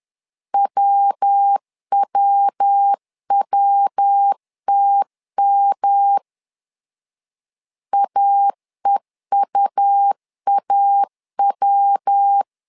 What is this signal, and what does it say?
Morse code, heard as a keyed tone, about 10 wpm, 800 Hz. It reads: WWWTM AEUAW
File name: LEZIONI / 3 MWRO / 3 MW 10wpm 1959